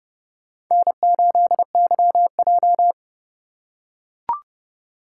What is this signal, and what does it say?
Morse code, heard as a keyed tone, about 32 wpm, 700 Hz. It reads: N8YJ
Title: USA Call Signs - 4 Characters - 3x Character Spacing - 30wpm